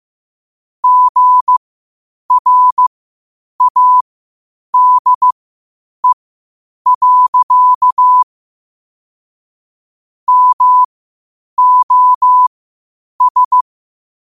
Morse code, heard as a keyed tone, about 15 wpm, spaced as for 8 wpm, 1000 Hz. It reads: GRADE. MOS
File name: qotdfiles / 8wpm